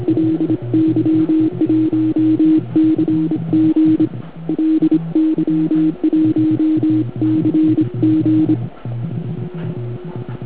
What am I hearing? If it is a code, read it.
Morse code, heard as a keyed tone, about 25 wpm, 320 Hz. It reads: LY1CG LY1CG